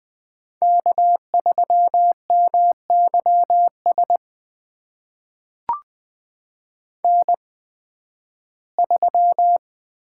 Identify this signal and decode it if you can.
Morse code, heard as a keyed tone, about 20 wpm, spaced as for 13 wpm, 700 Hz. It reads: K3MYS N3
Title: USA Call Signs - 5 Characters - 8x Character Spacing - 20wpm